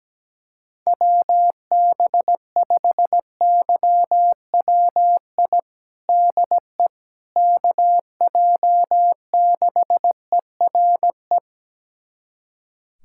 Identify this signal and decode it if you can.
Morse code, heard as a keyed tone, about 17 wpm, 700 Hz. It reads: WB5YWI DE KJ6ERE
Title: USA Callsign DE Callsign - 6 Characters - 17wpm